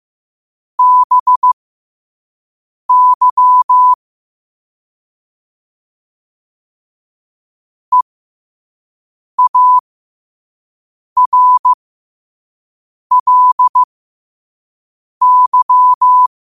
Morse code, heard as a keyed tone, about 15 wpm, spaced as for 5 wpm, 1000 Hz. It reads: BY EARLY
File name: qotdfiles / 5wpm